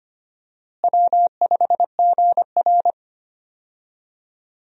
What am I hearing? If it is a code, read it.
Morse code, heard as a keyed tone, about 25 wpm, 700 Hz. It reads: W5GR
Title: USA Call Signs - 4 Characters - Repeated 3x - 25wpm